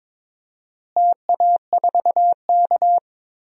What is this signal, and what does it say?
Morse code, heard as a keyed tone, about 22 wpm, 700 Hz. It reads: TA4K
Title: DXCC-100 Call Signs - 4 Characters - Repeated 2x - 22wpm